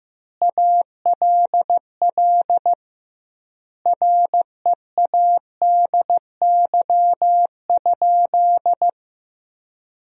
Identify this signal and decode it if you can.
Morse code, heard as a keyed tone, about 15 wpm, 700 Hz. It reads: ALL READY?
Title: 2x Word-Spacing - Sentences from Top 400 Words - 15wpm